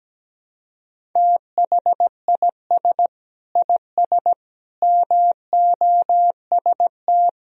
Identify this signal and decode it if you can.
Morse code, heard as a keyed tone, about 17 wpm, 700 Hz. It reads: THIS IS MOST
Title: Sentences from Top 1000 Words - Review Part 1 - 17wpm